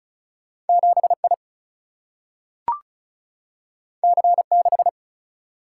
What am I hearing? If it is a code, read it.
Morse code, heard as a keyed tone, about 36 wpm, 700 Hz. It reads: 7I C6
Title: Single-Letter Single-Number 35wpm